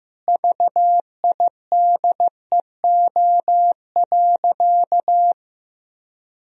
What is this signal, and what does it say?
Morse code, heard as a keyed tone, about 16 wpm, 700 Hz. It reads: VIDEO.
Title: Sentences from Top 1000 Words - Review Part 4 - 15wpm